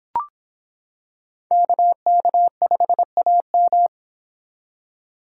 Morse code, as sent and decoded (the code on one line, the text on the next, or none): -.- -.- ..... .- --
KK5AM